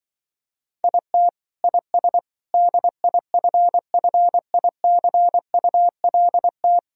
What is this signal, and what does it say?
Morse code, heard as a keyed tone, about 24 wpm, 700 Hz. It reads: IT IS DIFFICULT
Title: Speed-Racing - Sets of 4 Words 20wpm